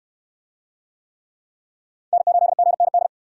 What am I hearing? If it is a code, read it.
Morse code, heard as a keyed tone, about 65 wpm, 690 Hz. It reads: N8GND